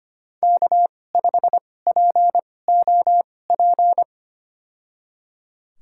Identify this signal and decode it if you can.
Morse code, heard as a keyed tone, about 25 wpm, spaced as for 18 wpm, 700 Hz. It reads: K5POP
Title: USA Call Signs - 2x Character Spacing - 25wpm